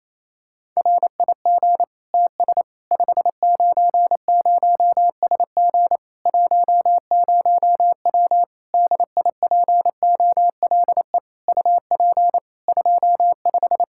Morse code, heard as a keyed tone, about 28 wpm, 700 Hz. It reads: RIG TS 590SG 10W DIPOLE UP 25
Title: K1USN SST QSO Practice 28wpm